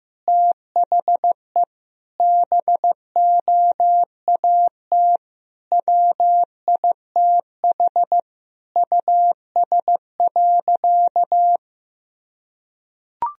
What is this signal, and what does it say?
Morse code, heard as a keyed tone, about 15 wpm, 700 Hz. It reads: THE BOAT WITH US.
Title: Sentences from Top 1000 Words - Review Part 3 - 15wpm